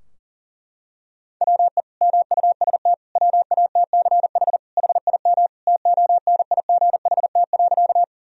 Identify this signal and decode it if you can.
Morse code, heard as a keyed tone, about 40 wpm, 700 Hz. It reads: WE MUST WATCH HIM TONIGHT.
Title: Sentences from Top 1000 Words - Review Part 3 - 40wpm